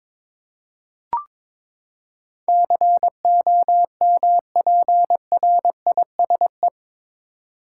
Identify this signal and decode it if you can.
Morse code, heard as a keyed tone, about 22 wpm, 700 Hz. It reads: COMPRISE
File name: practice / 22wpm / Eight Letter Words - 22wpm